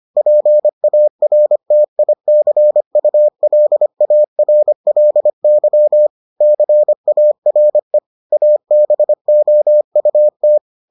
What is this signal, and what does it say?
Morse code, heard as a keyed tone, about 25 wpm, 600 Hz. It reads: PARTICULARLY CARE ABOUT